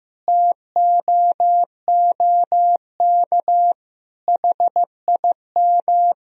Morse code, heard as a keyed tone, about 15 wpm, 700 Hz. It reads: TOOK HIM
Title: Sentences from Top 300 Words and Abbreviations 15wpm